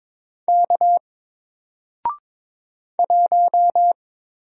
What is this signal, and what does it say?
Morse code, heard as a keyed tone, about 22 wpm, 700 Hz. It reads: K 1
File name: Single-Letter-Number-22wpm